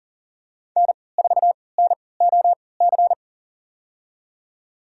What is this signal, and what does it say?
Morse code, heard as a keyed tone, about 40 wpm, spaced as for 23 wpm, 700 Hz. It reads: N4NOC
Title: USA Call Signs - 5 Characters - 3x Character Spacing - 40wpm